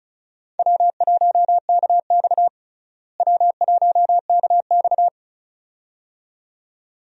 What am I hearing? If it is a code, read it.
Morse code, heard as a keyed tone, about 34 wpm, 700 Hz. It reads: W1KX W1KX
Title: USA Call Signs - 4 Characters - Repeated 2x - 35wpm